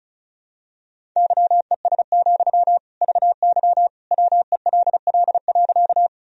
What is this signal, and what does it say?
Morse code, heard as a keyed tone, about 36 wpm, 700 Hz. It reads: YES, VY WELL.
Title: Sentences from Top 500 Words and Abbreviations 35wpm